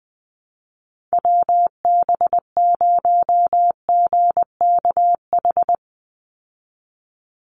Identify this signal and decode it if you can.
Morse code, heard as a keyed tone, about 20 wpm, 700 Hz. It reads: WB0GKH
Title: Call Signs Repeated 3x - 20wpm